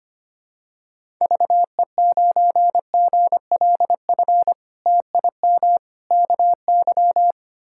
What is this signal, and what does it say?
Morse code, heard as a keyed tone, about 25 wpm, 700 Hz. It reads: VE9GLF TIM KY
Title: K1USN SST QSO Practice - 25wpm